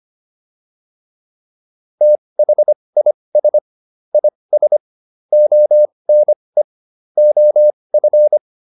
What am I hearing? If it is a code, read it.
Morse code, heard as a keyed tone, about 25 wpm, spaced as for 20 wpm, 600 Hz. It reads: THIS IS ONE OF
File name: WD304 20